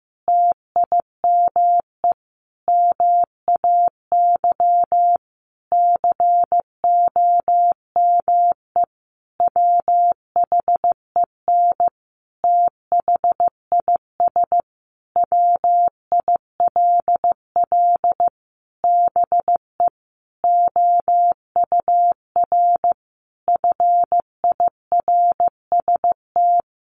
Morse code, heard as a keyed tone, about 15 wpm, 700 Hz. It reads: TIME MAY COME WHEN THIS WILL BE OUR FIRST
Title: Sentences from Top 1500 Words 15wpm